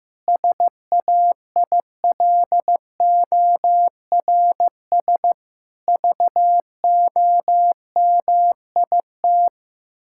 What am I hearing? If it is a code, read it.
Morse code, heard as a keyed tone, about 15 wpm, 700 Hz. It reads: SAILORS VOMIT
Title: Pangram Challenge v2 - 15wpm